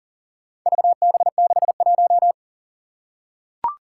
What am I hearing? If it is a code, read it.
Morse code, heard as a keyed tone, about 40 wpm, 700 Hz. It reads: VB61
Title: Four Challenging Letter-Number Combinations 40wpm